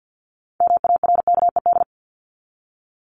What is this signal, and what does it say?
Morse code, heard as a keyed tone, about 50 wpm, 700 Hz. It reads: MARKED
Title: ED Suffix - Top 100 Words with Base Words 50wpm